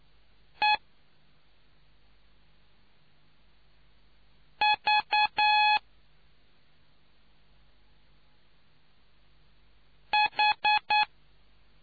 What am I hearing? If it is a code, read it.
Morse code, heard as a keyed tone, about 9 wpm, 840 Hz. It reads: E V H